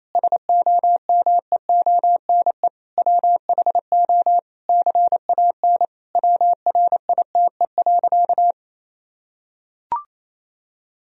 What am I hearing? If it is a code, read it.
Morse code, heard as a keyed tone, about 28 wpm, 700 Hz. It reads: SOMEONE WHO CAN WRITE.
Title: Sentences from Top 1000 Words - Review Part 1 - 28wpm